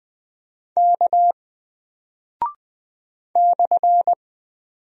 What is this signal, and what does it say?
Morse code, heard as a keyed tone, about 20 wpm, 700 Hz. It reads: K /
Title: Single Character (Letter:Number:Punctuation:Prosign) 20wpm